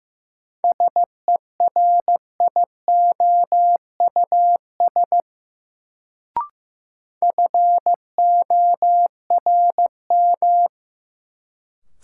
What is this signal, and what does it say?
Morse code, heard as a keyed tone, about 15 wpm, 700 Hz. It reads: SERIOUS FORM
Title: Sentences from Top 1100 Words - New Word Drill Alt - 15wpm